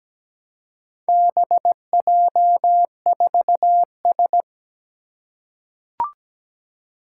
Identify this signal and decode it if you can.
Morse code, heard as a keyed tone, about 17 wpm, 700 Hz. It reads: BJ4S
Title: Four Challenging Letter-Number Combinations 17wpm